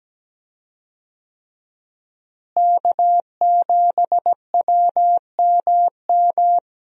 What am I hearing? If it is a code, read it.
Morse code, heard as a keyed tone, about 17 wpm, 700 Hz. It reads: K7WMM
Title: USA Call Signs - 5 Characters - Repeated 3x - 17wpm